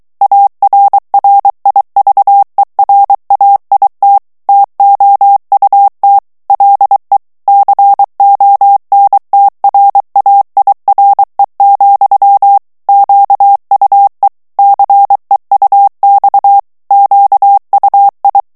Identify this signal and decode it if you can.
Morse code, heard as a keyed tone, about 23 wpm, 800 Hz. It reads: ARRIVERAIT TOUT LE CONTRAIRE, QUE CEUX QUI
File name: morse mp3 / FRANCAIS TEXTE CLAIR 2 VITESSE 22 MOTS MINUTES